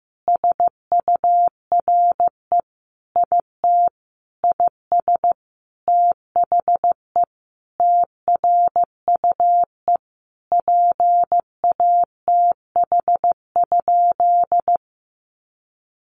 Morse code, heard as a keyed tone, about 14 wpm, 700 Hz. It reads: SURE IT IS THE TRUE PATH?
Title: Sentences from Top 1400 Words 15wpm